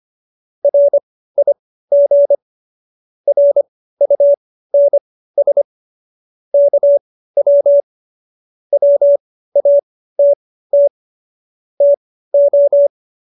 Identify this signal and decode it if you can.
Morse code, heard as a keyed tone, about 25 wpm, spaced as for 15 wpm, 570 Hz. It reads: RIG RUNS KW WATT TO